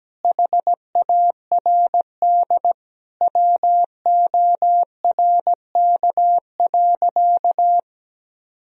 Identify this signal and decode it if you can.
Morse code, heard as a keyed tone, about 17 wpm, 700 Hz. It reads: HARD WORK.
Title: Sentences from Top 1000 Words - Review Part 3 - 17wpm